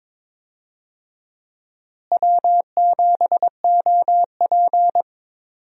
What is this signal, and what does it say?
Morse code, heard as a keyed tone, about 22 wpm, 700 Hz. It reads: W7OP
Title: USA Call Signs - 4 Characters - Repeated 2x - 22wpm